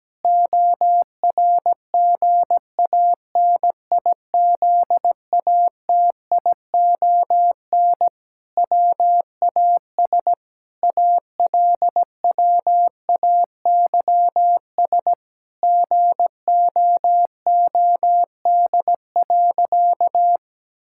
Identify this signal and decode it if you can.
Morse code, heard as a keyed tone, about 17 wpm, 700 Hz. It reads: ORGANIZATION WAS ALWAYS GOOD.
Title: Sentences from Top 1000 Words - Review Part 5 - 17wpm